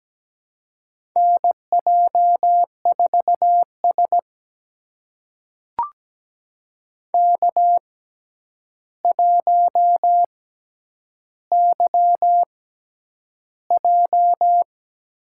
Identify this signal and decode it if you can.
Morse code, heard as a keyed tone, about 17 wpm, spaced as for 9 wpm, 700 Hz. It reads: NJ4S K1YJ
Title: USA Call Signs - 4 Characters - 6x Character Spacing - 17wpm